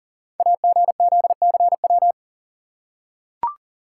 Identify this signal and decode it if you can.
Morse code, heard as a keyed tone, about 40 wpm, 700 Hz. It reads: AGZCW
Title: Code Groups - 5 Letters - 40wpm